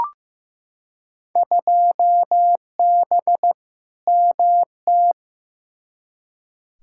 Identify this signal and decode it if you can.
Morse code, heard as a keyed tone, about 15 wpm, 700 Hz. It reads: 2B MT
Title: Field Day Exchanges Encore - 15wpm